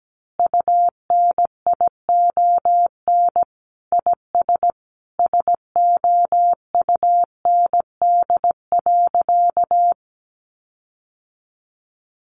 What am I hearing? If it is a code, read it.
Morse code, heard as a keyed tone, about 17 wpm, 700 Hz. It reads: UNION IS SOUND.
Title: Sentences from Top 1400 Words 17wpm